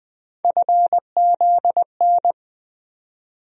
Morse code, heard as a keyed tone, about 20 wpm, 700 Hz. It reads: FZN
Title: USA Call Sign Postfixes - 3 Letters 20wpm